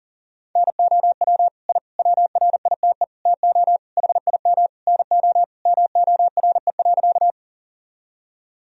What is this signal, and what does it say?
Morse code, heard as a keyed tone, about 37 wpm, 700 Hz. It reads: NOW I WRITE TO HIM NO MORE.
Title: Sentences from Top 1000 Words - Review Part 1 - 40wpm